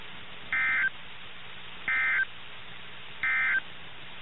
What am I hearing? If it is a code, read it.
Emergency Alert System (SAME) tones, heard as harsh buzzing bursts. It is heard from 0.4 to 3.7 s.
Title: NOAA Weather Radio - KIH61